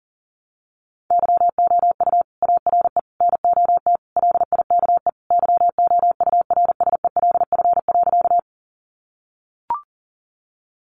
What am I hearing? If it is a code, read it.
Morse code, heard as a keyed tone, about 37 wpm, 700 Hz. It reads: YOU ARE NOT LIKE YOURSELF.
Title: Sentences from Top 600 Words 40wpm